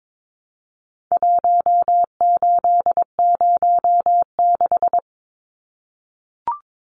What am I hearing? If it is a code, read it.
Morse code, heard as a keyed tone, about 22 wpm, 700 Hz. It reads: 1806
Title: Numbers - 4-Digit 22wpm